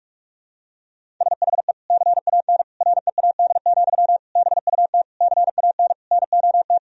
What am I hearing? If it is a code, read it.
Morse code, heard as a keyed tone, about 46 wpm, 700 Hz. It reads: SHE CAN READ, BUT CAN NOT
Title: Sentences from Top 1000 Words - Review Part 1 - 45wpm